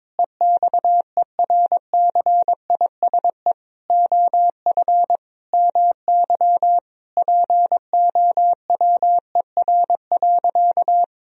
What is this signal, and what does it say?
Morse code, heard as a keyed tone, about 22 wpm, 700 Hz. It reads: EXERCISE OF MY POWER.
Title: Sentences from Top 1000 Words - Review Part 4 - 22wpm